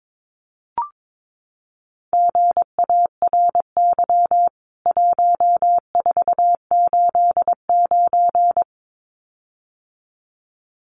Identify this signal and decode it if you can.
Morse code, heard as a keyed tone, about 22 wpm, 700 Hz. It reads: GARY 1489
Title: CWT Exchange 22wpm